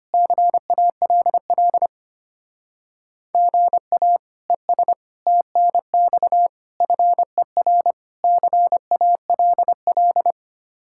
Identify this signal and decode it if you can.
Morse code, heard as a keyed tone, about 31 wpm, 700 Hz. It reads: CALL GA ES TNX FER CALL
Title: Speed-Racing Common QSO Phrases - 25wpm